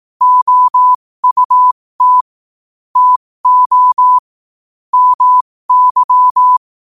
Morse code, heard as a keyed tone, about 18 wpm, spaced as for 15 wpm, 1000 Hz. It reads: OUT TO MY
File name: qotdfiles / 15wpm